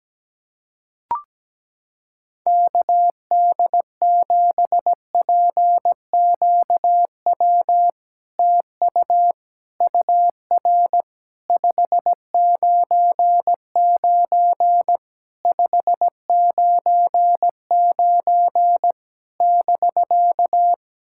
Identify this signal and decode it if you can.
Morse code, heard as a keyed tone, about 17 wpm, 700 Hz. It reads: KD7PQW TU UR 599 599 <BK>
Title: SOTA QSO Practice 17wpm